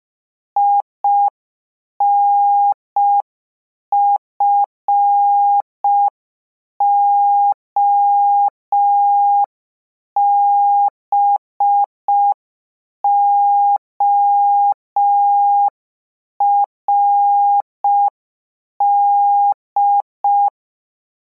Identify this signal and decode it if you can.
Morse code, heard as a keyed tone, about 5 wpm, 810 Hz. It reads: INFOBORD